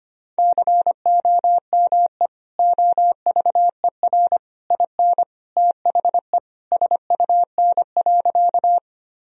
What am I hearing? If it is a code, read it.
Morse code, heard as a keyed tone, about 25 wpm, 700 Hz. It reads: COME OVER IN THE SUN.
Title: Sentences from Top 1000 Words - Review Part 1 - 25wpm